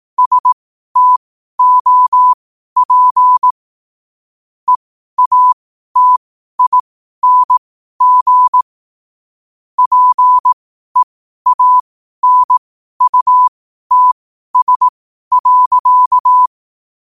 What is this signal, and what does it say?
Morse code, heard as a keyed tone, about 18 wpm, spaced as for 12 wpm, 1000 Hz. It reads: STOP EATING PEANUTS.
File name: qotdfiles / 12wpm